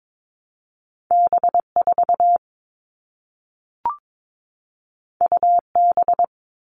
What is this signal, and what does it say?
Morse code, heard as a keyed tone, about 22 wpm, 700 Hz. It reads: B4 UB
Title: All Cut Numbers - 2-Digit 22wpm